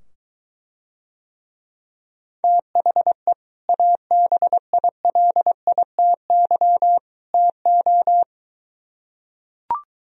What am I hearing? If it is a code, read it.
Morse code, heard as a keyed tone, about 23 wpm, 700 Hz. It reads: THE ABILITY TO